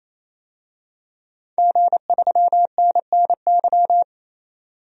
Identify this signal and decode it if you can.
Morse code, heard as a keyed tone, about 27 wpm, 690 Hz. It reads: G3NNY
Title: DXCC-100 Call Signs - 5 Characters - 28wpm